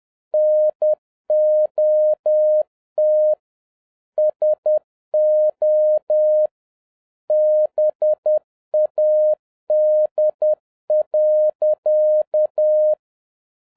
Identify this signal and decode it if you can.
Morse code, heard as a keyed tone, about 10 wpm, 610 Hz. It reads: NOT SO BAD.
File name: 2011-07-mp3 / n56 BACK FROM A BIG ACCIDENT 10wpm